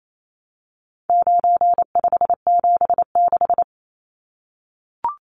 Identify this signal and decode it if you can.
Morse code, heard as a keyed tone, about 28 wpm, 700 Hz. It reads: 9576
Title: Numbers - 4-Digit 28wpm